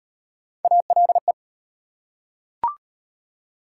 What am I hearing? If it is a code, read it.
Morse code, heard as a keyed tone, about 33 wpm, 700 Hz. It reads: ALE